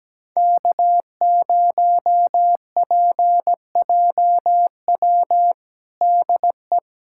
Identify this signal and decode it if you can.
Morse code, heard as a keyed tone, about 17 wpm, 700 Hz. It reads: K0PJW DE
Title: USA Callsign DE Callsign - 5 Characters - 17wpm